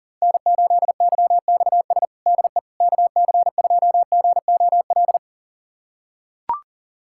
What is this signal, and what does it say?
Morse code, heard as a keyed tone, about 40 wpm, 700 Hz. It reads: N8YXS DE KC2GOL
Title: USA Callsign DE Callsign 40wpm